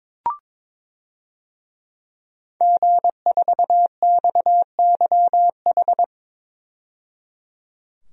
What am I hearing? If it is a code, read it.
Morse code, heard as a keyed tone, about 22 wpm, 700 Hz. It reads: G4XYH
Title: DXCC-100 Call Signs - 5 Characters - 22wpm